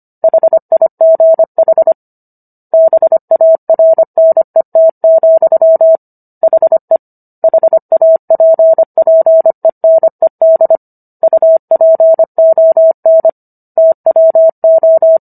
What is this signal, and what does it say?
Morse code, heard as a keyed tone, about 25 wpm, 650 Hz. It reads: HIGH BARNET, HE HAPPENED UPON TWO
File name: Book One / 16-Chapter 16 - 25 WPM